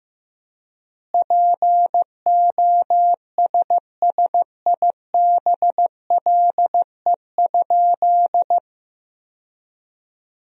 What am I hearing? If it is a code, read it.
Morse code, heard as a keyed tone, about 15 wpm, 700 Hz. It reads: POSSIBLE?
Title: 4x Word-Spacing - Sentences from Top 500 Words - 15wpm